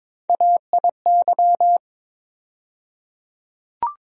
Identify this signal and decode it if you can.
Morse code, heard as a keyed tone, about 22 wpm, 700 Hz. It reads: AIY
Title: USA Call Sign Postfixes - 3 Letters 22wpm